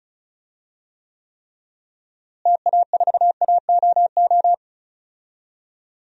Morse code, heard as a keyed tone, about 34 wpm, 700 Hz. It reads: TA4AOO
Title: DXCC-100 Call Signs - 6 Characters - Repeated 3x - 35wpm